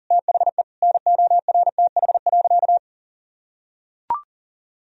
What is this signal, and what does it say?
Morse code, heard as a keyed tone, about 37 wpm, 700 Hz. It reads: THE NORTH.
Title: Sentences from Top 1000 Words - Review Part 1 - 40wpm